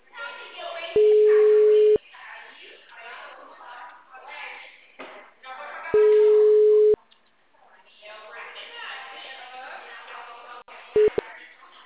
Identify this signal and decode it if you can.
Ringback tone (Europe, 425 Hz), with speech.